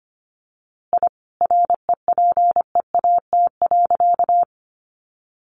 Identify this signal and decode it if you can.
Morse code, heard as a keyed tone, about 25 wpm, 700 Hz. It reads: I REPEAT.
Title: A Sound Of Thunder 25wpm